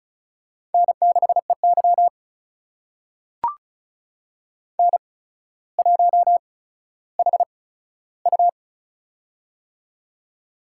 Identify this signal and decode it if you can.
Morse code, heard as a keyed tone, about 36 wpm, spaced as for 15 wpm, 700 Hz. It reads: N6EY N1HU
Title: USA Call Signs - 4 Characters - 8x Character Spacing - 35wpm